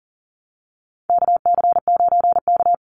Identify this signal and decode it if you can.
Morse code, heard as a keyed tone, about 40 wpm, 700 Hz. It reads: KC9K
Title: Call Signs - 40wpm